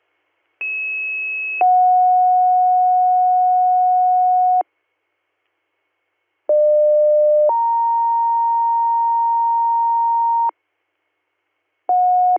Two-tone page, heard as a one-second tone, then a three-second tone, about 2600 then 730 Hz, about 600 then 930 Hz. It is heard from 0.6 to 10.5 s.
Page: bradford-pa